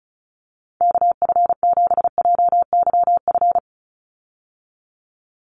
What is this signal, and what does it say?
Morse code, heard as a keyed tone, about 34 wpm, 700 Hz. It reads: KF7JYF
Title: USA Call Signs 35wpm